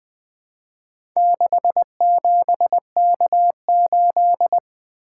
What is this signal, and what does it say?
Morse code, heard as a keyed tone, about 20 wpm, 700 Hz. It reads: 67K8
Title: Four Letter-Number Combinations - 20wpm